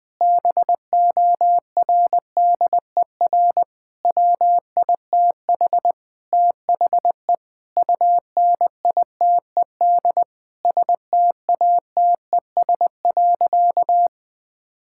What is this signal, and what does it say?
Morse code, heard as a keyed tone, about 21 wpm, 700 Hz. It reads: BORDER WITH THE UNITED STATES.
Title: Sentences with 4-Word Phrases - 20wpm